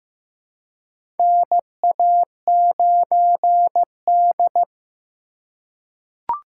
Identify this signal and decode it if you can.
Morse code, heard as a keyed tone, about 15 wpm, 700 Hz. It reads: NA9D
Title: USA Call Signs - 4 Characters - 3x Character Spacing - 15wpm